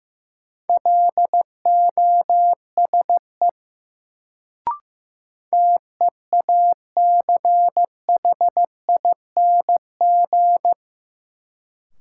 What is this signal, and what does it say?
Morse code, heard as a keyed tone, about 16 wpm, 700 Hz. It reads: LOSE TEACHING
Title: Sentences from Top 1100 Words - New Word Drill - 15wpm